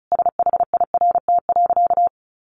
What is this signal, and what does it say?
Morse code, heard as a keyed tone, about 34 wpm, 700 Hz. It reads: SHIRT.
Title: Additional Easy Phrases 35wpm